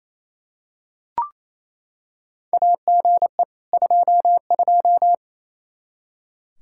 Morse code, heard as a keyed tone, about 28 wpm, 700 Hz. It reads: AGE 22
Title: Practice QSO - Exchange 3 - More Info - 28wpm